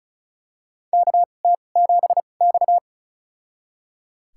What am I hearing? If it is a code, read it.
Morse code, heard as a keyed tone, about 36 wpm, spaced as for 25 wpm, 700 Hz. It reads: KT7X
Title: USA Call Signs - 4 Characters - 2x Character Spacing - 35wpm